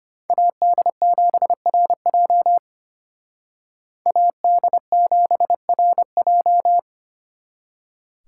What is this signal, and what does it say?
Morse code, heard as a keyed tone, about 32 wpm, 700 Hz. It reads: AD7RJ AD7RJ